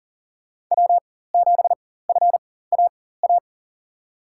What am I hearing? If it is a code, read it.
Morse code, heard as a keyed tone, about 40 wpm, spaced as for 19 wpm, 700 Hz. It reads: W7FAA
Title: USA Call Signs - 4x Character Spacing - 40wpm